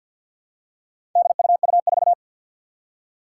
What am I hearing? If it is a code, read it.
Morse code, heard as a keyed tone, about 52 wpm, 700 Hz. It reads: DUU4
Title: Four Challenging Letter-Number Combinations 50wpm